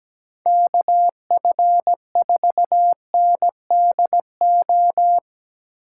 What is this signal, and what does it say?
Morse code, heard as a keyed tone, about 20 wpm, 700 Hz. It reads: KF4NDO
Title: Speed Racing - USA 2x3 Call Signs 17wpm